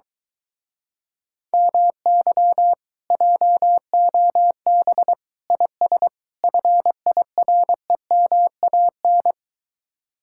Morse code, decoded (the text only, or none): MY JOB IS FIREMAN